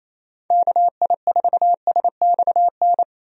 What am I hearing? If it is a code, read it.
Morse code, heard as a keyed tone, about 27 wpm, 700 Hz. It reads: KI4SXN
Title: USA Call Signs - 6 Characters - Repeated 3x - 28wpm